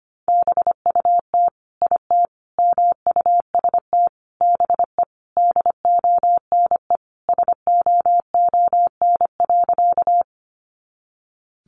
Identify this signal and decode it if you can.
Morse code, heard as a keyed tone, about 25 wpm, 700 Hz. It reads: BUT IT MUST BE DONE SOON.
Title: Sentences from Top 500 Words 25wpm